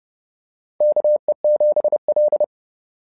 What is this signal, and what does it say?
Morse code, heard as a keyed tone, about 30 wpm, 600 Hz. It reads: KE7L
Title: SOTA QSO Practice 30wpm